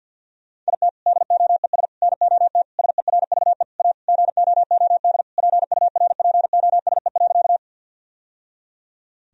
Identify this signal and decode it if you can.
Morse code, heard as a keyed tone, about 50 wpm, 700 Hz. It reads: IT DOES NOT SERVE A GOOD PURPOSE.
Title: Sentences from Top 1000 Words - Review Part 2 - 50wpm